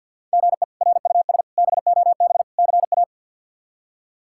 Morse code, heard as a keyed tone, about 47 wpm, 700 Hz. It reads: GE RUS BOB CA
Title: K1USN SST QSO Practice - 50wpm